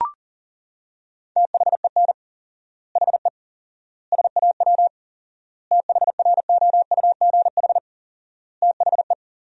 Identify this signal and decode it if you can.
Morse code, heard as a keyed tone, about 40 wpm, 700 Hz. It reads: THEN HE SAW THROUGH THE
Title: 4x Word-Spacing - Sentences from Top 400 Words - 40wpm